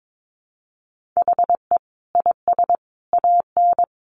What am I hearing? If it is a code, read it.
Morse code, heard as a keyed tone, about 22 wpm, 700 Hz. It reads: HE IS AN
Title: Sentences from Top 1500 Words 22wpm